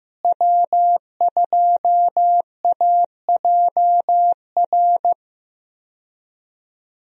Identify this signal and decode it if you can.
Morse code, heard as a keyed tone, about 16 wpm, 700 Hz. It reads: W2AJR
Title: USA Call Signs - 5 Characters - Repeated 3x - 15wpm